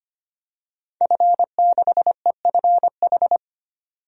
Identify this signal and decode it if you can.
Morse code, heard as a keyed tone, about 25 wpm, 690 Hz. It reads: F6EFH
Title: DXCC-100 Call Signs - 5 Characters - 25wpm